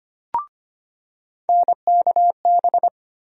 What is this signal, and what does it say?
Morse code, heard as a keyed tone, about 25 wpm, 700 Hz. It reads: NKB
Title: USA Call Sign Postfixes - 3 Letters 25wpm